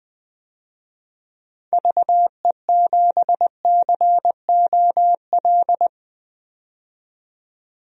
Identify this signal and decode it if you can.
Morse code, heard as a keyed tone, about 20 wpm, 700 Hz. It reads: VE7COL
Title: DXCC-100 Call Signs - 6 Characters - 20wpm